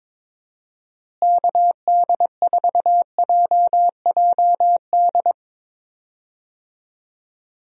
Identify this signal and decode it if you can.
Morse code, heard as a keyed tone, about 22 wpm, 700 Hz. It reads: KD4JJD
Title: USA Call Signs - 6 Characters - Repeated 2x - 22wpm